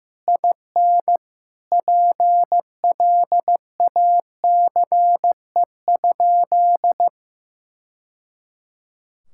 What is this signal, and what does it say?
Morse code, heard as a keyed tone, about 15 wpm, 700 Hz. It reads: IN PLACE?
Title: Sentences from Top 1000 Words - Review Part 6 - 15wpm